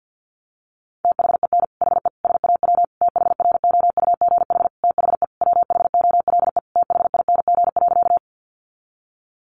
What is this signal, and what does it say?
Morse code, heard as a keyed tone, about 48 wpm, 700 Hz. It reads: THEN HE SAW THROUGH THE WHOLE THING.
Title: Sentences from Top 400 Words 50wpm